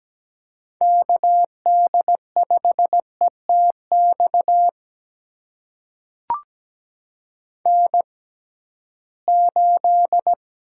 Morse code, heard as a keyed tone, about 17 wpm, spaced as for 13 wpm, 700 Hz. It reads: KD5ETX N8
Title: USA Call Signs - 6x Character Spacing - 17wpm